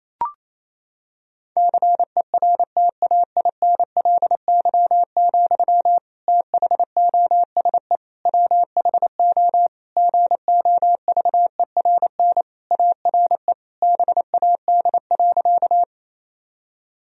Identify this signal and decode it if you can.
Morse code, heard as a keyed tone, about 28 wpm, 700 Hz. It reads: CERTAINLY, THOSE WHO GOVERN ARE BAD.
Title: Sentences from Top 1000 Words - Review Part 2 - 28wpm